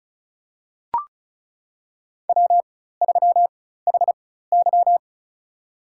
Morse code, heard as a keyed tone, about 34 wpm, spaced as for 16 wpm, 700 Hz. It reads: W3HY
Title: USA Call Signs - 4x Character Spacing - 35wpm